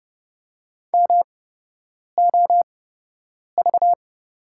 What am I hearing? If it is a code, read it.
Morse code, heard as a keyed tone, about 30 wpm, spaced as for 8 wpm, 700 Hz. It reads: MOV
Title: 8x Character Spacing - Top 300 Words - 30wpm